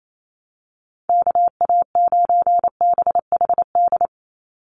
Morse code, heard as a keyed tone, about 28 wpm, 700 Hz. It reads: KA9BHD
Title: Call Signs - 28wpm